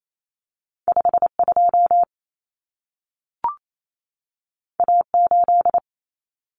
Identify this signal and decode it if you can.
Morse code, heard as a keyed tone, about 28 wpm, 700 Hz. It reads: 52 A8
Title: Common Cut Numbers - 2-Digit 28wpm